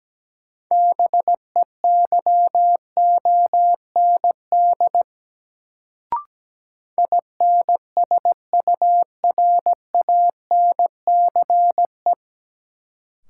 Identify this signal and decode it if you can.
Morse code, heard as a keyed tone, about 17 wpm, 700 Hz. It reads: BEYOND INSURANCE
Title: Sentences from Top 1100 Words - New Word Drill Alt - 17wpm